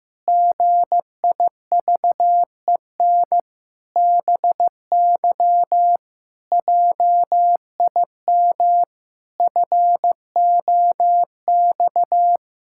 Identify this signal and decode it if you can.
Morse code, heard as a keyed tone, about 15 wpm, 700 Hz. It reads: GIVEN BY JIM FOX
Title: Pangram Challenge v2 - 15wpm